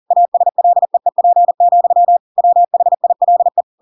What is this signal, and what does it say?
Morse code, heard as a keyed tone, about 40 wpm, 700 Hz. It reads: ASLEEP, WHILE
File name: Pinocchio / Pinocchio ch8 40 40 ik1hge